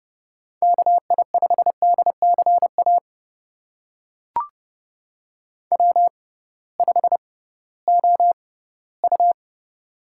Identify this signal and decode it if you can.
Morse code, heard as a keyed tone, about 30 wpm, spaced as for 18 wpm, 700 Hz. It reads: KI5DCA W5OU
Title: USA Call Signs - 6x Character Spacing - 30wpm